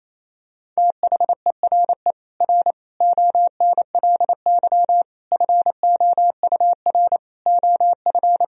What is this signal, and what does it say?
Morse code, heard as a keyed tone, about 28 wpm, 700 Hz. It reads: THERE R ONLY FOUR OF
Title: Sentences from Top 300 Words and Abbreviations 28wpm